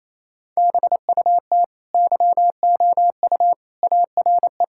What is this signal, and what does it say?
Morse code, heard as a keyed tone, about 28 wpm, 700 Hz. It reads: BUT YOU ARE
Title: Sentences from Top 1000 Words - Review Part 5 - 28wpm